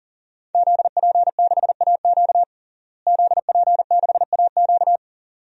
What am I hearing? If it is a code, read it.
Morse code, heard as a keyed tone, about 40 wpm, 700 Hz. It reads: ZP6AQ ZP6AQ
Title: DXCC-100 Call Signs - 5 Characters - Repeated 2x - 40wpm